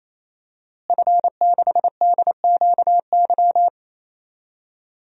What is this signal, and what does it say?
Morse code, heard as a keyed tone, about 27 wpm, 700 Hz. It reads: F6DQY
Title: DXCC-100 Call Signs - 5 Characters - Repeated 3x - 28wpm